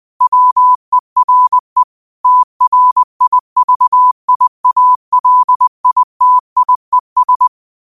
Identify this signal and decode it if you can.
Morse code, heard as a keyed tone, about 20 wpm, 1000 Hz. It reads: WERE TRIVIALITIES
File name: qotdfiles / 20wpm